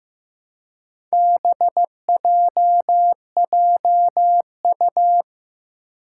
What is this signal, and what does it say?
Morse code, heard as a keyed tone, about 15 wpm, 700 Hz. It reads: BJJU
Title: Four Challenging Letter-Number Combinations 15wpm